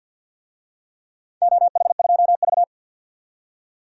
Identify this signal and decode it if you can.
Morse code, heard as a keyed tone, about 52 wpm, 700 Hz. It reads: OH2V
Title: DXCC-100 Call Signs - 4 Characters - 50wpm